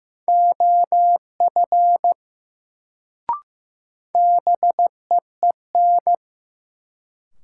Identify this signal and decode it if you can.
Morse code, heard as a keyed tone, about 15 wpm, 700 Hz. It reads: OF BEEN
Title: Sentences from Top 100 Words - New Word Drill - 15wpm